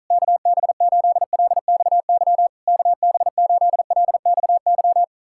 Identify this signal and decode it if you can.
Morse code, heard as a keyed tone, about 48 wpm, 700 Hz. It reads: KB8LXY KB8LXY